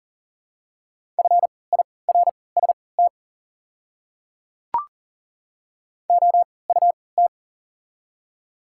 Morse code, heard as a keyed tone, about 40 wpm, spaced as for 23 wpm, 690 Hz. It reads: FIRST OUT
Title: 3x Character Spacing - Top 200 Words - 40wpm